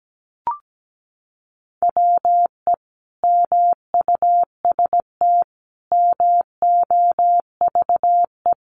Morse code, heard as a keyed tone, about 17 wpm, 700 Hz. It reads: WE MUST MOVE
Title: Sentences from Top 1100 Words 17wpm